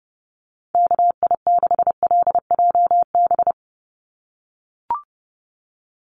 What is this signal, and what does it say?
Morse code, heard as a keyed tone, about 30 wpm, 700 Hz. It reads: KI6LJB
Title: USA Call Signs 30wpm